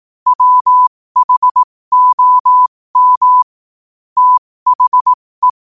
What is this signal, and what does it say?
Morse code, heard as a keyed tone, about 18 wpm, spaced as for 15 wpm, 1000 Hz. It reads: WHOM THE
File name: qotdfiles / 15wpm